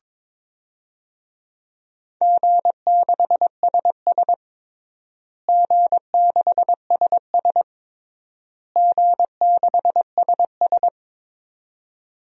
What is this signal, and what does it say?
Morse code, heard as a keyed tone, about 22 wpm, 700 Hz. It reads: G6SS G6SS G6SS
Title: DXCC-100 Call Signs - 4 Characters - Repeated 3x - 22wpm